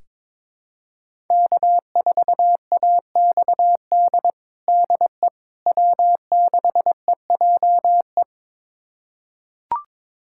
Morse code, heard as a keyed tone, about 22 wpm, 700 Hz. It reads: K4AXD DE W6EJE
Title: USA Callsign DE Callsign 22wpm